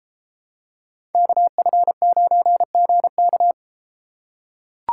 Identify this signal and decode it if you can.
Morse code, heard as a keyed tone, about 33 wpm, 700 Hz. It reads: KF9GK